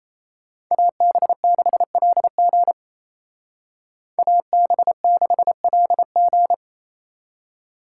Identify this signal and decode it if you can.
Morse code, heard as a keyed tone, about 34 wpm, 700 Hz. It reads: AB6LG AB6LG
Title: Speed Racing - USA 2x2 Call Signs 28wpm